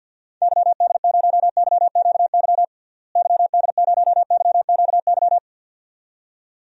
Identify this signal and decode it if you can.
Morse code, heard as a keyed tone, about 49 wpm, 700 Hz. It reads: YD0YQY YD0YQY
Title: DXCC-100 Call Signs - 6 Characters - Repeated 2x - 50wpm